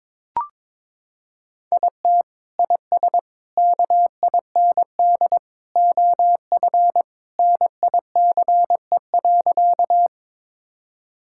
Morse code, heard as a keyed tone, about 22 wpm, 700 Hz. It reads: IT IS KIND OF NICE.
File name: practice / 22wpm / Sentences with 2-Word Phrases - 22wpm